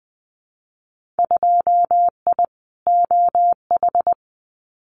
Morse code, heard as a keyed tone, about 20 wpm, 700 Hz. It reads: 2I OH
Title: Winter Field Day Exchanges 20wpm